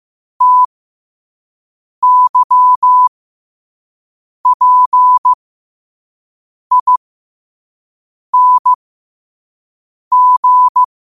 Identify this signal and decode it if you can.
Morse code, heard as a keyed tone, about 15 wpm, spaced as for 5 wpm, 1000 Hz. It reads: TYPING
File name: qotdfiles / 5wpm